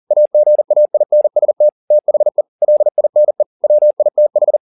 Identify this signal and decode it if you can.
Morse code, heard as a keyed tone, about 40 wpm, 600 Hz. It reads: AGAINST THE LINE WITH
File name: the old man and the sea 2 40wpm